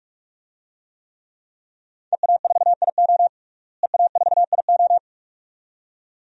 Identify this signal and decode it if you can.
Morse code, heard as a keyed tone, about 46 wpm, 700 Hz. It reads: EA4IO EA4IO
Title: DXCC-100 Call Signs - 5 Characters - Repeated 2x - 45wpm